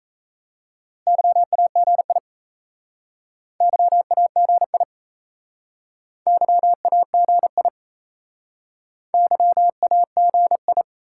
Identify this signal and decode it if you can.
Morse code, heard as a keyed tone, about 35 wpm, 700 Hz. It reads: YAGI YAGI YAGI YAGI
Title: Speed-Racing Top 180 QSO Elements 28wpm